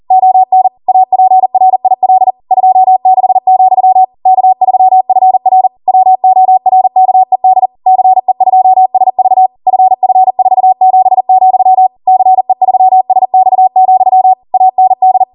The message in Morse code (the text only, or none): ON APRIL 26, K3FR WORKED CE2SV FF47, CE3SX, AND